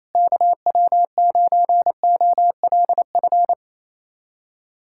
Morse code, heard as a keyed tone, about 28 wpm, 700 Hz. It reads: KW9OLF
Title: USA Callsign DE Callsign 28wpm